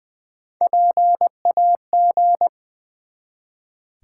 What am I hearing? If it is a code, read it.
Morse code, heard as a keyed tone, about 20 wpm, 700 Hz. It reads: PAG
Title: Build A Word - 1 Letter At A Time - 4-Letter Words - 20wpm